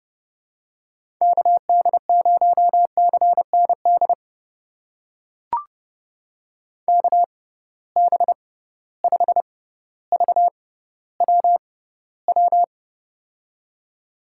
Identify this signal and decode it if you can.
Morse code, heard as a keyed tone, about 31 wpm, spaced as for 15 wpm, 700 Hz. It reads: KD0CND KB5VWW
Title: USA Call Signs - 6 Characters - 6x Character Spacing - 30wpm